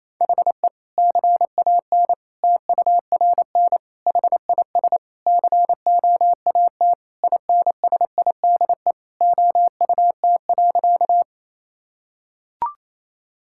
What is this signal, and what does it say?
Morse code, heard as a keyed tone, about 28 wpm, 700 Hz. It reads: HE CAN TURN HIS COAT INSIDE OUT.
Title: Sentences from Top 1600 Words 28wpm